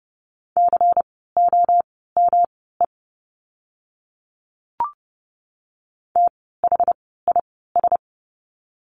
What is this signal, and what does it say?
Morse code, heard as a keyed tone, about 30 wpm, spaced as for 17 wpm, 700 Hz. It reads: COME THIS
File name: course / 30:10wpm / 230 Lesson 37f - Letter x - Words 30wpm